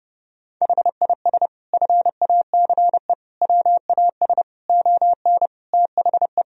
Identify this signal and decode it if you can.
Morse code, heard as a keyed tone, about 30 wpm, 700 Hz. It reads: HIS FACE WAS ON THE
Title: Sentences from Top 1000 Words - Review Part 2 - 30wpm